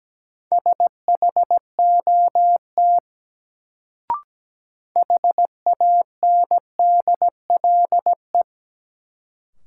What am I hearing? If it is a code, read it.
Morse code, heard as a keyed tone, about 17 wpm, 700 Hz. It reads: SHOT HANDLE
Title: Sentences from Top 1100 Words - New Word Drill - 17wpm